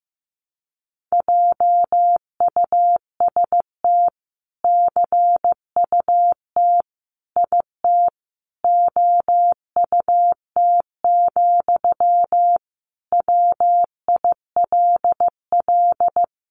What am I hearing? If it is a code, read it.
Morse code, heard as a keyed tone, about 15 wpm, 700 Hz. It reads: JUST CUT IT OUT, WILL
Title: Sentences from Top 300 Words 15wpm